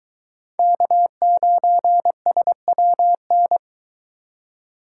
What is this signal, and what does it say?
Morse code, heard as a keyed tone, about 23 wpm, 700 Hz. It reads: K9SWN